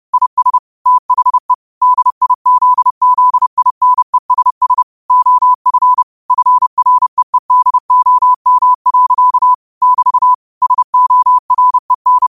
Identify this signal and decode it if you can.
Morse code, heard as a keyed tone, about 30 wpm, 1000 Hz. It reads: IS THE DIZZINESS OF FREEDOM. <BT> SOREN